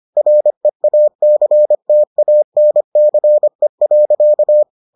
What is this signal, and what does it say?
Morse code, heard as a keyed tone, about 25 wpm, 600 Hz. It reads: REACTANCE.